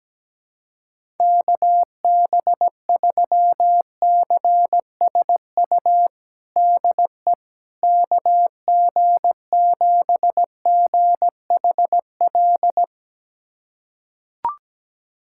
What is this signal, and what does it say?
Morse code, heard as a keyed tone, about 17 wpm, 700 Hz. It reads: KB3CSU DE KG7GHL
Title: USA Callsign DE Callsign - 6 Characters - 17wpm